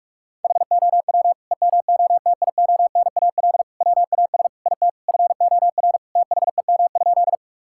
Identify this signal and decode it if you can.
Morse code, heard as a keyed tone, about 41 wpm, 700 Hz. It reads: HOW EMOTIONAL WAS IT FOR THEM?
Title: Sentences from Top 1000 Words - Review Part 5 - 45wpm